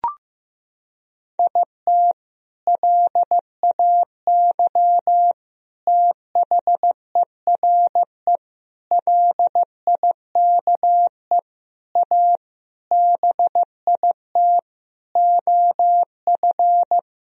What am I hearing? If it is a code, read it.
Morse code, heard as a keyed tone, about 15 wpm, 700 Hz. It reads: IT LAY THERE LIKE A BIT OF
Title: Sentences from Top 1000 Words - Review Part 3 - 15wpm